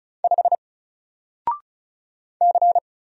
Morse code, heard as a keyed tone, about 34 wpm, 700 Hz. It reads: 5 C
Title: Single Letter-Number Alt 35wpm